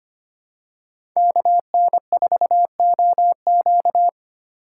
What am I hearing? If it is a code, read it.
Morse code, heard as a keyed tone, about 25 wpm, 700 Hz. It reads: KN4OQ
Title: USA Call Signs - 4x Character Spacing - 25wpm